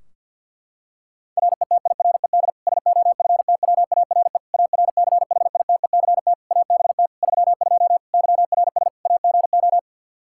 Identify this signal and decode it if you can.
Morse code, heard as a keyed tone, about 50 wpm, 700 Hz. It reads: RETIRED SOFTWARE ARCHITECT ABT 32 YRS AGO
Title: Practice QSO - Exchange 3 - More Info - 50wpm